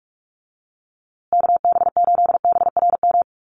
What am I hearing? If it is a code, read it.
Morse code, heard as a keyed tone, about 44 wpm, 700 Hz. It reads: KB8BRM
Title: USA Call Signs Repeated 2x 45wpm